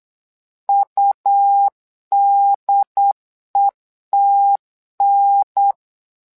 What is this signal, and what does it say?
Morse code, heard as a keyed tone, about 8 wpm, 800 Hz. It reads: UDETN
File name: LEZIONI / 2 UHDN / 2 UHDN 8wpm2232